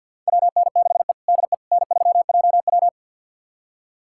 Morse code, heard as a keyed tone, about 51 wpm, 690 Hz. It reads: WN6E DE N3JW
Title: USA Callsign DE Callsign - 4 Characters - 50wpm